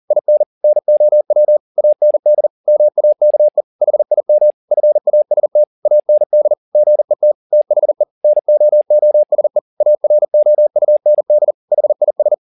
Morse code, heard as a keyed tone, about 40 wpm, 600 Hz. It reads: IN NOW AND MAKE HIM FAST AND GET THE NOOSE AROUND HIS
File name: the old man and the sea 2 40wpm